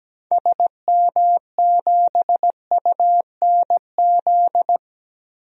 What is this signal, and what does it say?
Morse code, heard as a keyed tone, about 17 wpm, 700 Hz. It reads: SM7UNZ
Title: DXCC-100 Call Signs - 6 Characters - Repeated 2x - 17wpm